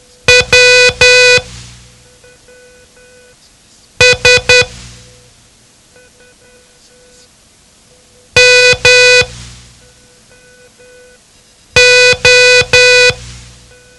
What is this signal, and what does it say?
Morse code, heard as a keyed tone, about 10 wpm, spaced as for 3 wpm, 500 Hz. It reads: WS MO
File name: ham radio / CodeQuick / Code Quick 07